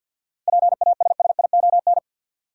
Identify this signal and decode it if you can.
Morse code, heard as a keyed tone, about 46 wpm, 700 Hz. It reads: PASSION
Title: Seven Letter Words - 50wpm